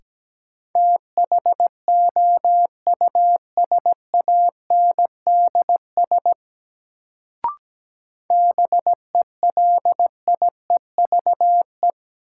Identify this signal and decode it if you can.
Morse code, heard as a keyed tone, about 17 wpm, 700 Hz. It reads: THOUSANDS BELIEVE
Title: Sets of 4 Words - New Word Drill - 17wpm